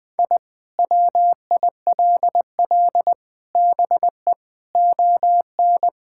Morse code, heard as a keyed tone, about 21 wpm, 700 Hz. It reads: I WILL BE ON
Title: Sentences from Top 1000 Words - Review Part 5 - 20wpm